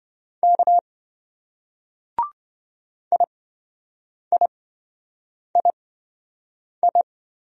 Morse code, heard as a keyed tone, about 27 wpm, spaced as for 7 wpm, 700 Hz. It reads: K IIII